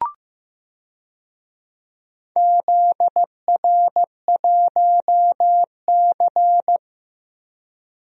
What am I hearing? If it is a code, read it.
Morse code, heard as a keyed tone, about 15 wpm, 700 Hz. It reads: ZR1C
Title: DXCC-100 Call Signs - 4 Characters - Repeated 3x - 15wpm